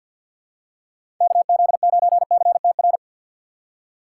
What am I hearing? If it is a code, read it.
Morse code, heard as a keyed tone, about 53 wpm, 690 Hz. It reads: KZ9CTR
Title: USA Call Signs - 6 Characters - 8x Character Spacing - 50wpm